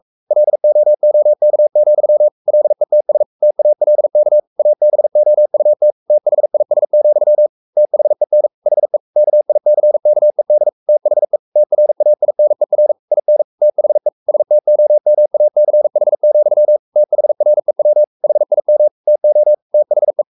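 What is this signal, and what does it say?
Morse code, heard as a keyed tone, about 44 wpm, 600 Hz. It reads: LOOK, LETS TALK ABOUT THIS, THEN HE KICKED THE TRAINER IN THE STOMACH, THREW HIM TO THE
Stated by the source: Morse code at 43 wpm sending Ken Follett's novel Never in English, from CW med Gnister.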